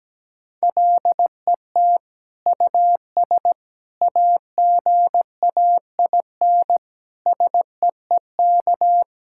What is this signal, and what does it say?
Morse code, heard as a keyed tone, about 17 wpm, 700 Hz. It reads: LET US AGAIN SEEK
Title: Sentences from Top 1600 Words 17wpm